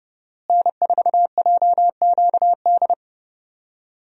Morse code, heard as a keyed tone, about 30 wpm, 700 Hz. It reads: N4JQD
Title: USA Call Signs - 5 Characters - Repeated 3x - 30wpm